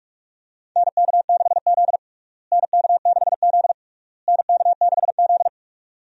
Morse code, heard as a keyed tone, about 46 wpm, 700 Hz. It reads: NK6Z NK6Z NK6Z
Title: USA Call Signs - 4 Characters - Repeated 3x - 45wpm